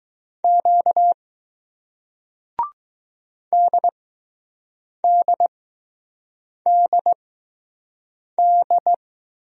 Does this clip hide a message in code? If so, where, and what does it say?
Yes: Morse code from 0.4 to 8.9 s, heard as a keyed tone, about 20 wpm, spaced as for 6 wpm, 700 Hz. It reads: Q DDDD